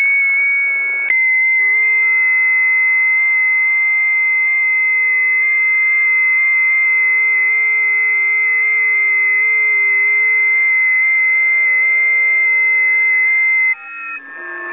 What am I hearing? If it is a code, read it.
FT8.
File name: tests / 20m busy / test 34